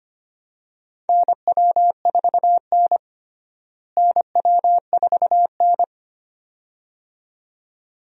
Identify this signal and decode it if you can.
Morse code, heard as a keyed tone, about 25 wpm, 700 Hz. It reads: NW4N NW4N
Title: USA Call Signs - 4 Characters - Repeated 2x - 25wpm